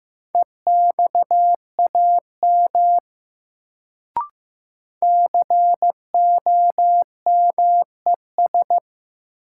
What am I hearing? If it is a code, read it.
Morse code, heard as a keyed tone, about 16 wpm, 700 Hz. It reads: EXAM COMES
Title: Sentences from Top 1000 Words - Review Part 5 - New Word Drill - 15wpm